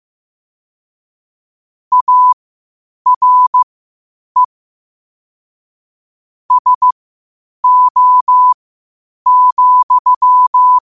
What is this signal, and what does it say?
Morse code, heard as a keyed tone, about 15 wpm, spaced as for 8 wpm, 1000 Hz. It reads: ARE SO,
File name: qotdfiles / 8wpm